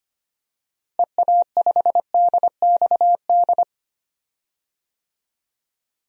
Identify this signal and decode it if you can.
Morse code, heard as a keyed tone, about 25 wpm, 700 Hz. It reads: EA5DXD
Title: DXCC-100 Call Signs - 6 Characters - Repeated 3x - 25wpm